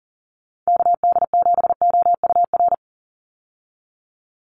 Morse code, heard as a keyed tone, about 40 wpm, 700 Hz. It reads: KD7OUR
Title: USA Call Signs 40wpm